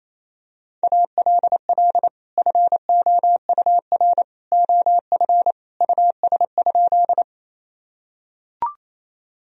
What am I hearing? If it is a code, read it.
Morse code, heard as a keyed tone, about 28 wpm, 700 Hz. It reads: ALL FOUR OF US?
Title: Sentences from Top 1000 Words - Review Part 1 - 28wpm